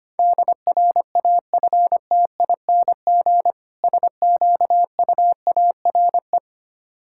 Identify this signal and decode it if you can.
Morse code, heard as a keyed tone, about 26 wpm, 700 Hz. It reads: DRAFTING SQUARE
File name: practice / 25wpm / Difficult Words - Speed-Racing with Hints - 25wpm